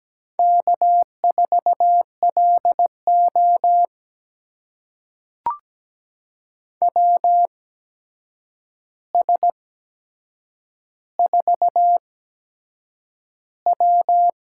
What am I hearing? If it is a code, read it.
Morse code, heard as a keyed tone, about 17 wpm, spaced as for 7 wpm, 700 Hz. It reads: K4LO WS4W
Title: USA Call Signs - 4 Characters - 8x Character Spacing - 17wpm